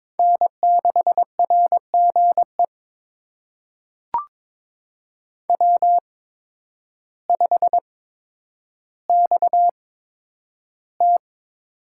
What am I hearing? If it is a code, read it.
Morse code, heard as a keyed tone, about 22 wpm, spaced as for 10 wpm, 700 Hz. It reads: N6RGE W5XT
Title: USA Call Signs - 5 Characters - 8x Character Spacing - 22wpm